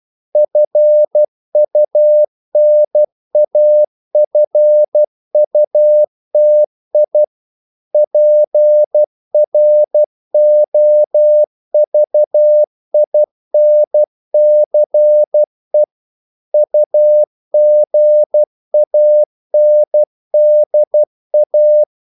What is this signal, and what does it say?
Morse code, heard as a keyed tone, about 12 wpm, 600 Hz. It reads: FUNAFUTI PROVINCE UGANDA
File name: CW / Practice / NationalCapitals12